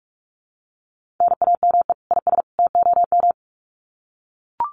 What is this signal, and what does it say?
Morse code, heard as a keyed tone, about 44 wpm, 700 Hz. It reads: NAME IS TOM
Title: Common QSO Phrases 45wpm